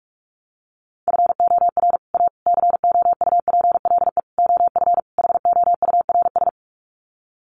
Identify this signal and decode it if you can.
Morse code, heard as a keyed tone, about 44 wpm, 700 Hz. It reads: FOR A COUPLE OF HOURS
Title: Sets of 5 Words Encore2 45wpm